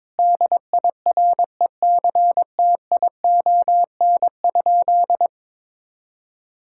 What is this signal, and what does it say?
Morse code, heard as a keyed tone, about 22 wpm, 700 Hz. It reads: DIRECTION?
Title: Sentences from Top 1000 Words - Review Part 5 - 22wpm